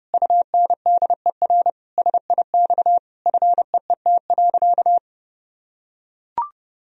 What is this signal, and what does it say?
Morse code, heard as a keyed tone, about 30 wpm, 700 Hz. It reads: UNDER SIX FEET.
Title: Sentences from Top 1000 Words - Review Part 2 - 30wpm